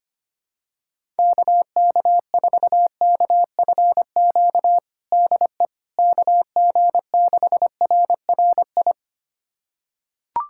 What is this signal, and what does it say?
Morse code, heard as a keyed tone, about 25 wpm, 700 Hz. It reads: KK4KFQ DE KG6RRI
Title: USA Callsign DE Callsign - 6 Characters - 25wpm